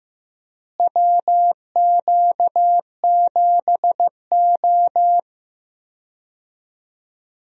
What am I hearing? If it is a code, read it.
Morse code, heard as a keyed tone, about 15 wpm, 700 Hz. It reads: WQ7O
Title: USA Call Signs - 4 Characters - Repeated 2x - 15wpm